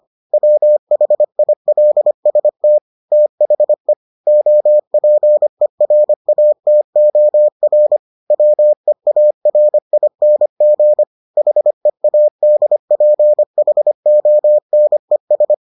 Morse code, heard as a keyed tone, about 25 wpm, 600 Hz. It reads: WHILST THE OPERATOR WEARING HEADPHONES